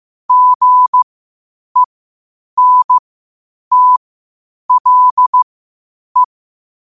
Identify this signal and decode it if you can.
Morse code, heard as a keyed tone, about 15 wpm, spaced as for 8 wpm, 1000 Hz. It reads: GENTLE
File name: qotdfiles / 8wpm